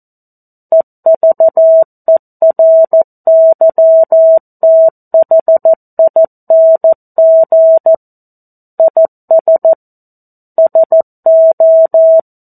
Morse code, heard as a keyed tone, about 14 wpm, 650 Hz. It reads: EVERYTHING IS SO
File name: 02-Chapter-02 14 WPM